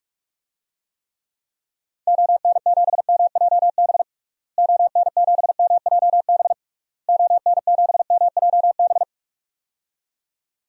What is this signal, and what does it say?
Morse code, heard as a keyed tone, about 46 wpm, 700 Hz. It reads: ON7MJB ON7MJB ON7MJB
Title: DXCC-100 Call Signs - 6 Characters - Repeated 3x - 45wpm